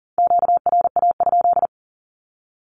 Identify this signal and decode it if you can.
Morse code, heard as a keyed tone, about 38 wpm, 700 Hz. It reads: QRA?
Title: Q-Codes 40wpm